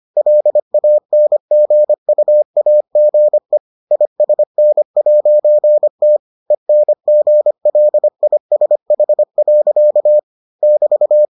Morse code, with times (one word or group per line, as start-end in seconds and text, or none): 0.2-3.6 LANGUAGE
3.9-6.2 ISN'T
6.5-10.2 ENGLISH.
10.6-11.3 <BT>